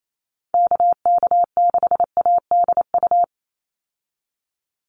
Morse code, heard as a keyed tone, about 28 wpm, 700 Hz. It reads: KK6ADU
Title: USA Call Signs 28wpm